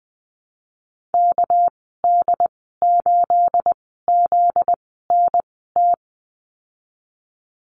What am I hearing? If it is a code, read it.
Morse code, heard as a keyed tone, about 20 wpm, spaced as for 15 wpm, 700 Hz. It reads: KD8ZNT
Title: 245 Lesson 39g - Letter z - Call Signs 20wpm